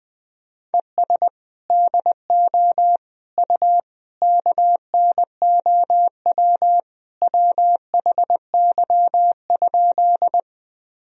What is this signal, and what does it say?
Morse code, heard as a keyed tone, about 20 wpm, 700 Hz. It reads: ES DO U KNOW WHY?
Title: Sentences from Top 200 Words and Abbreviations 20wpm